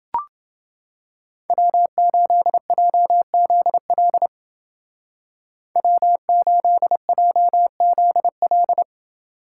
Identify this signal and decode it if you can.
Morse code, heard as a keyed tone, about 29 wpm, 700 Hz. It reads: W8JZL W8JZL